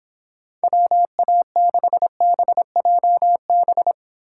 Morse code, heard as a keyed tone, about 30 wpm, 700 Hz. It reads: WA6BJB